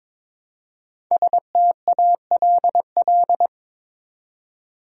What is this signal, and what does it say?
Morse code, heard as a keyed tone, about 22 wpm, 700 Hz. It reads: STALL
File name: 5 Letter Words - First 3 Letters Given - 22wpm